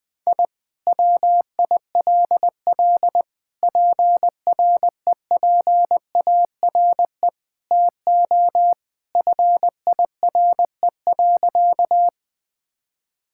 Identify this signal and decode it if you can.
Morse code, heard as a keyed tone, about 20 wpm, 700 Hz. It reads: I WILL PREPARE TO FIRE.
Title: Sentences from Top 1000 Words - Review Part 6 - 20wpm